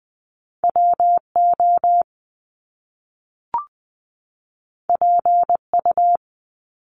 Morse code, heard as a keyed tone, about 20 wpm, 700 Hz. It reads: WO PU
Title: Top Two Letter Combinations in Words 20wpm